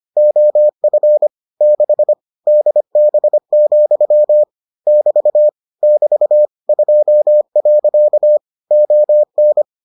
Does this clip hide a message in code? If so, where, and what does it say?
Yes: Morse code from 0.2 to 9.6 s, heard as a keyed tone, about 25 wpm, 600 Hz. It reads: OF 6 DB, <BT> <BT> 2. ON